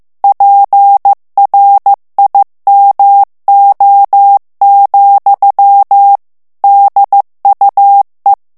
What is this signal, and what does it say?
Morse code, heard as a keyed tone, about 15 wpm, 800 Hz. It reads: PRIMO, DUE
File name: morse mp3 / ITALIEN texte clair 2 vitesse 14 mots minute